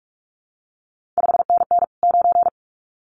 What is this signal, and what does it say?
Morse code, heard as a keyed tone, about 46 wpm, 700 Hz. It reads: 5NN 9
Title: CQ WW DX Contest Exchange Encore - 45wpm